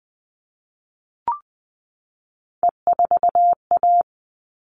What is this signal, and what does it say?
Morse code, heard as a keyed tone, about 20 wpm, 700 Hz. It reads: E4A
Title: All Cut Numbers - 3-Digit 20wpm